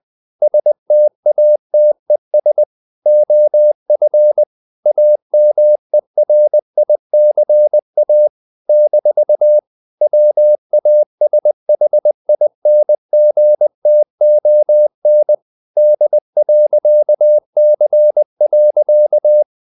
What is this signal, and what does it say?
Morse code, heard as a keyed tone, about 20 wpm, 600 Hz. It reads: STATES OF AMERICA - WASHINGTON D.C.
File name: CW / Practice / NationalCapitals20